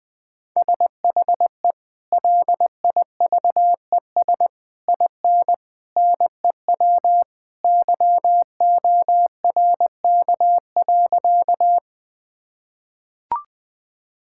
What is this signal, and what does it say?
Morse code, heard as a keyed tone, about 20 wpm, 700 Hz. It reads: SHE LIVES IN NEW YORK.
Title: Sentences with 3-Word Phrases - Spoken Beginning - 20wpm